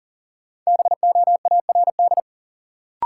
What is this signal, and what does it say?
Morse code, heard as a keyed tone, about 40 wpm, 700 Hz. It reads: BOARD
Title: Sentences from Top 700 Words - New Word Drill - 40wpm